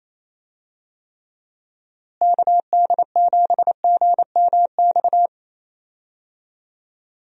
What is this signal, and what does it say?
Morse code, heard as a keyed tone, about 28 wpm, 700 Hz. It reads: KD7GMX
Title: USA Call Signs - 6 Characters - Repeated 3x - 28wpm